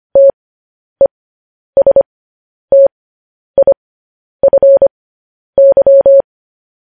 Morse code, heard as a keyed tone, about 25 wpm, spaced as for 10 wpm, 550 Hz. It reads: TESTIFY